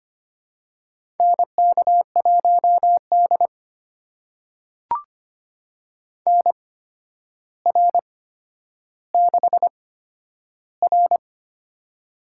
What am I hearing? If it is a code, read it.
Morse code, heard as a keyed tone, about 25 wpm, spaced as for 11 wpm, 700 Hz. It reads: NK1D NR6R
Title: USA Call Signs - 8x Character Spacing - 25wpm